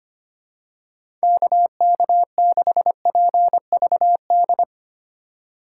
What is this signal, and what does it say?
Morse code, heard as a keyed tone, about 25 wpm, 700 Hz. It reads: KK6PVD
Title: USA Call Signs - 6 Characters - 4x Character Spacing - 25wpm